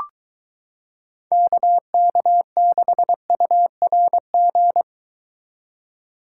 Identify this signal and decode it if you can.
Morse code, heard as a keyed tone, about 22 wpm, 700 Hz. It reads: KK6URG